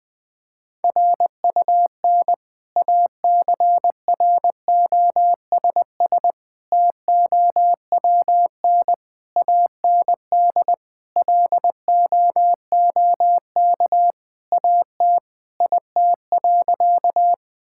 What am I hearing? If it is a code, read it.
Morse code, heard as a keyed tone, about 21 wpm, 700 Hz. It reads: RUN ACROSS TOWN AND LOOK AT IT.
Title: Sentences from Top 1000 Words - Review Part 5 - 20wpm